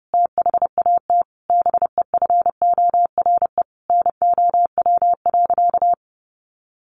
Morse code, heard as a keyed tone, about 30 wpm, 700 Hz. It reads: THAT BEFORE NOW.
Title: Sentences from Top 1300 Words 30wpm